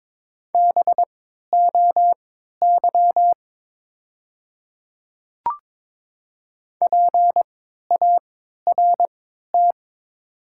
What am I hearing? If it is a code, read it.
Morse code, heard as a keyed tone, about 23 wpm, spaced as for 13 wpm, 700 Hz. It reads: BOY PART
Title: 3x Character Spacing - Top 500 Words - 22wpm